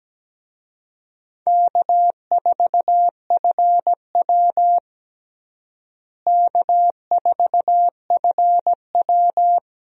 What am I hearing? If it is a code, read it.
Morse code, heard as a keyed tone, about 17 wpm, 700 Hz. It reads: K4FW K4FW
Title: USA Call Signs - 4 Characters - Repeated 2x - 17wpm